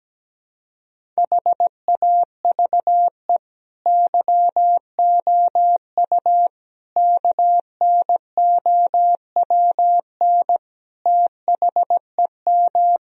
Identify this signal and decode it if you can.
Morse code, heard as a keyed tone, about 17 wpm, 700 Hz. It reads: HAVE YOU KNOWN THEM
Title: Sentences from Top 1000 Words - Review Part 3 - 17wpm